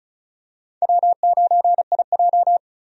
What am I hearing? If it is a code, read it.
Morse code, heard as a keyed tone, about 36 wpm, 700 Hz. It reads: W9IJ
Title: USA Call Signs - 4 Characters - 35wpm